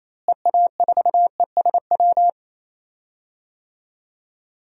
Morse code, heard as a keyed tone, about 28 wpm, 700 Hz. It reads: EA4ESW
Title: DXCC-100 Call Signs - 6 Characters - Repeated 2x - 28wpm